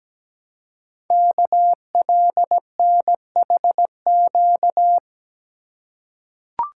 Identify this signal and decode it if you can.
Morse code, heard as a keyed tone, about 17 wpm, 700 Hz. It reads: KLNHQ
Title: Code Groups - 5 Letters - 17wpm